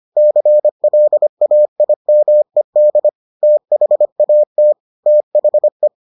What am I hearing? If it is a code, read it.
Morse code, heard as a keyed tone, about 25 wpm, 600 Hz. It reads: CLAIMED THAT THE